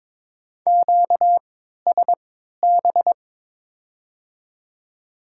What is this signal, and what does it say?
Morse code, heard as a keyed tone, about 22 wpm, 700 Hz. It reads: Q S B
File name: Top 90 QSO Elements - 3x Character Spacing - 22wpm